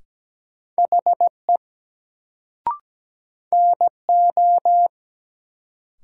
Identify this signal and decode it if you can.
Morse code, heard as a keyed tone, about 17 wpm, 700 Hz. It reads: HE NO
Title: Sentences from Top 100 Words - New Word Drill - 17wpm